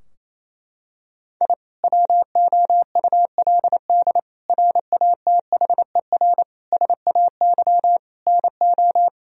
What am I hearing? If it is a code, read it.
Morse code, heard as a keyed tone, about 27 wpm, 700 Hz. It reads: I WOULD RATHER SAY NO
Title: Sentences from Top 1000 Words - Review Part 4 - 28wpm